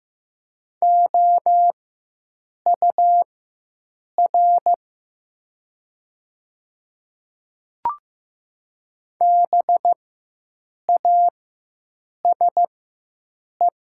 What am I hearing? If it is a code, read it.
Morse code, heard as a keyed tone, about 15 wpm, spaced as for 7 wpm, 700 Hz. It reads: OUR BASE
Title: 4x Character Spacing - Top 500 Words - 15wpm